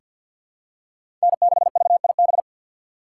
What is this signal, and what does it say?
Morse code, heard as a keyed tone, about 50 wpm, 700 Hz. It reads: N6VIB
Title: USA Call Signs - 5 Characters - 2x Character Spacing - 50wpm